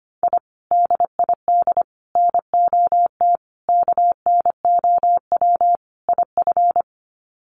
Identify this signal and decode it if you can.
Morse code, heard as a keyed tone, about 25 wpm, 700 Hz. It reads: I DID NOT KNOW IF
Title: Sets of 5 Words Encore 25wpm